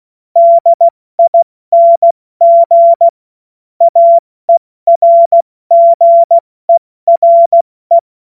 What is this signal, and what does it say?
Morse code, heard as a keyed tone, about 16 wpm, spaced as for 14 wpm, 680 Hz. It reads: DING AERGERE
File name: Morsekurs / Witze / Witz 06 Februar 2024 70 F80